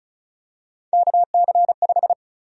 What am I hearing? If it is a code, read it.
Morse code, heard as a keyed tone, about 36 wpm, 700 Hz. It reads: KC5
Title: USA Call Sign Prefixes - 35wpm